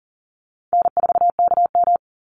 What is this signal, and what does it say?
Morse code, heard as a keyed tone, about 40 wpm, 700 Hz. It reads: N4KM
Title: Easy Call Signs - 40wpm